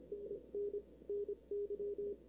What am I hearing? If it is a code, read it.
Morse code, heard as a keyed tone, about 28 wpm, spaced as for 24 wpm, 410 Hz. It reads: NN NY
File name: arrl cw dx 2005